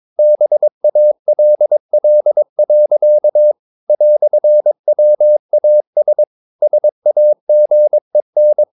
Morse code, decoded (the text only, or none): BALL. "WAS SAGEN